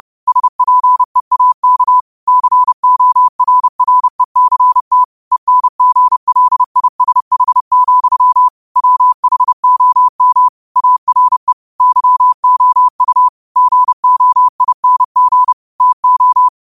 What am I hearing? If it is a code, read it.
Morse code, heard as a keyed tone, about 30 wpm, 1000 Hz. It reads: SPEAK CORRECT ENGLISH, WHOM ARE YOU GOING TO